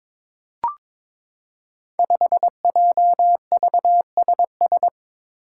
Morse code, heard as a keyed tone, about 22 wpm, 700 Hz. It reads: 5JVSS
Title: Five Challenging Letter-Number Combinations 22wpm